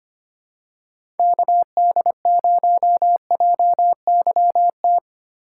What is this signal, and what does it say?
Morse code, heard as a keyed tone, about 25 wpm, 700 Hz. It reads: KD0JYT
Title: USA Call Signs - 6 Characters - 4x Character Spacing - 25wpm